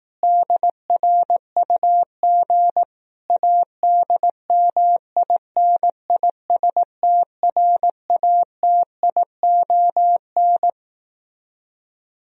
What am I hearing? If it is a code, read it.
Morse code, heard as a keyed tone, about 18 wpm, 700 Hz. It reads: DRUG ADMINISTRATION
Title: Speed-Racing - Sets of 5 Words 15wpm